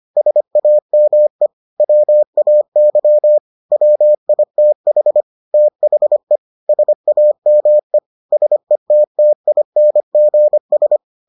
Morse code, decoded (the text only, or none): SAME WAY WITH THE SAME SETTINGS